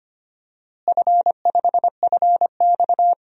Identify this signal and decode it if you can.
Morse code, heard as a keyed tone, about 25 wpm, 700 Hz. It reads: F5FX
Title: DXCC-100 Call Signs - 4 Characters - Repeated 3x - 25wpm